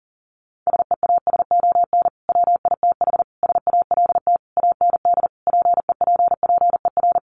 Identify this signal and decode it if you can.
Morse code, heard as a keyed tone, about 40 wpm, 700 Hz. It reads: SEASON WITH SALT AND PEPPER
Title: Sets of 5 Words 40wpm